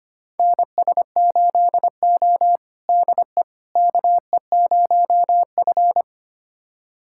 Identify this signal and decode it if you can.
Morse code, heard as a keyed tone, about 25 wpm, 700 Hz. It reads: NS8O DE KE0F
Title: USA Callsign DE Callsign - 4 Characters - 25wpm